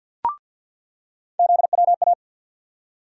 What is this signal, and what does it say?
Morse code, heard as a keyed tone, about 47 wpm, 700 Hz. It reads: ZWA
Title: USA Call Sign Postfixes - 3 Letters 50wpm